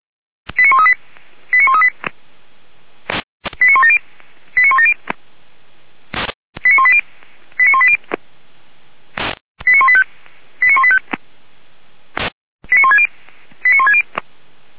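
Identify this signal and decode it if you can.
Five-tone selective call (ZVEI), heard as quick runs of five tones, tones of about 80 ms. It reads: 89138 89138 89170 89170 89180 89180 89185 89185 89160 89160
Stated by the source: Five-tone ZVEI selective call, received by SDR.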